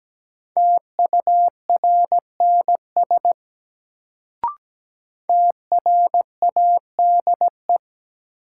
Morse code, heard as a keyed tone, about 17 wpm, 700 Hz. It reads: TURNS TRADE
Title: Sets of 5 Words Encore - New Word Drill - 17wpm